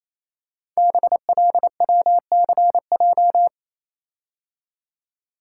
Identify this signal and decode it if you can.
Morse code, heard as a keyed tone, about 28 wpm, 700 Hz. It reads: BLWCJ
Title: Code Groups - 5 Letters - 28wpm